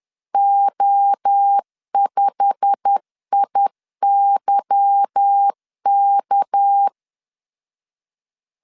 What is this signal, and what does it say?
Morse code, heard as a keyed tone, about 10 wpm, 800 Hz. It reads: O5IYK
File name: LEZIONI / 8 YJ45 / 8 YJ45 10wpm 2549